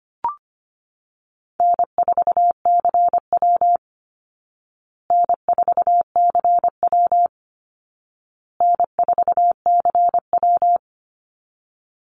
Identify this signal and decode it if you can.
Morse code, heard as a keyed tone, about 25 wpm, 700 Hz. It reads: N4CW N4CW N4CW
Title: Easy Call Signs Repeated 3x - 25wpm